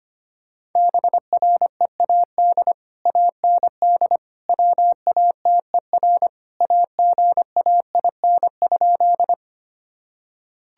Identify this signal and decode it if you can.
Morse code, heard as a keyed tone, about 25 wpm, 700 Hz. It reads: BREAD AND WATER AGAIN?
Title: Sentences from Top 1600 Words 25wpm